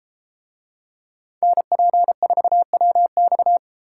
Morse code, heard as a keyed tone, about 31 wpm, 700 Hz. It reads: NP4WX